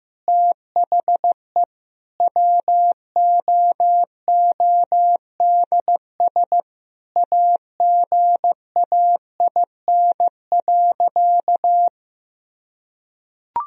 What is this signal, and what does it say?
Morse code, heard as a keyed tone, about 15 wpm, 700 Hz. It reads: THE WOODS AGAIN.
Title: Sentences from Top 1000 Words - Review Part 2 - 15wpm